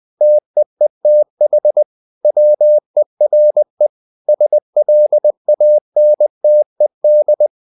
Morse code, heard as a keyed tone, about 20 wpm, 600 Hz. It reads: TEETH WERE SLANTED
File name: the old man and the sea part 2 20wpm